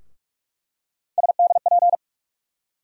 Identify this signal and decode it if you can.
Morse code, heard as a keyed tone, about 46 wpm, 700 Hz. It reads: SDP
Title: USA Call Sign Postfixes - 3 Letters 45wpm